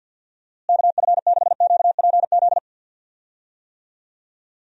Morse code, heard as a keyed tone, about 52 wpm, 700 Hz. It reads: KF6QPZ
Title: USA Call Signs - 6 Characters - Repeated 3x - 50wpm